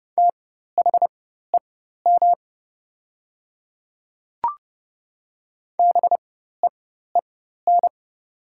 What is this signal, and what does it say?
Morse code, heard as a keyed tone, about 30 wpm, spaced as for 14 wpm, 700 Hz. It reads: THEM BEEN
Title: 4x Character Spacing - Top 200 Words - 30wpm